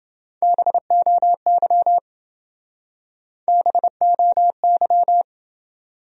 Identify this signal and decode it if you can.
Morse code, heard as a keyed tone, about 28 wpm, 700 Hz. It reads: BOY BOY